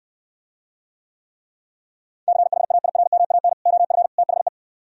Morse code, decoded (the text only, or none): CHAIRMAN OF THE